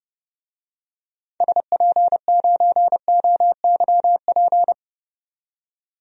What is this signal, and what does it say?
Morse code, heard as a keyed tone, about 32 wpm, 700 Hz. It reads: SP9OYP
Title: DXCC-100 Call Signs - 6 Characters - 30wpm